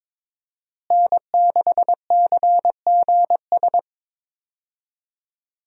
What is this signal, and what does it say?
Morse code, heard as a keyed tone, about 22 wpm, 700 Hz. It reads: N6CGS
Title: USA Call Signs - 5 Characters - Repeated 3x - 22wpm